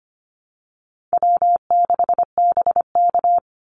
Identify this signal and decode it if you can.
Morse code, heard as a keyed tone, about 25 wpm, 700 Hz. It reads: W6BK
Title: Call Signs - 25wpm